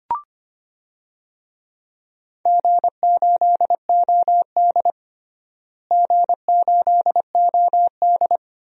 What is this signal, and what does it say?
Morse code, heard as a keyed tone, about 25 wpm, 700 Hz. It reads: G8OD G8OD
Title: DXCC-100 Call Signs - 4 Characters - Repeated 3x - 25wpm